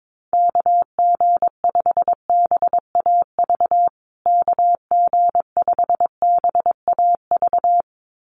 Morse code, heard as a keyed tone, about 22 wpm, 700 Hz. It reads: KG5BAV KG5BAV
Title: USA Call Signs Repeated 2x 22wpm